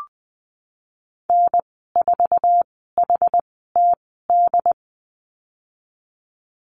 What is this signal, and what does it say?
Morse code, heard as a keyed tone, about 20 wpm, spaced as for 14 wpm, 700 Hz. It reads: N4HTD
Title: 069 Lesson 12g - Letter d - Call Signs 20wpm